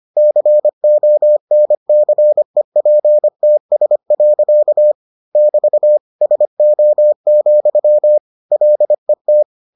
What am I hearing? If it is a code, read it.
Morse code, heard as a keyed tone, about 25 wpm, 600 Hz. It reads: CONCEPTS. <BT> SO, LET